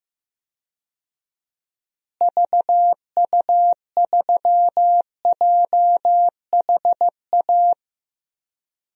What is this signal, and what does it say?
Morse code, heard as a keyed tone, about 15 wpm, 700 Hz. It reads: VU3JHA
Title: DXCC-100 Call Signs - 6 Characters - Repeated 2x - 15wpm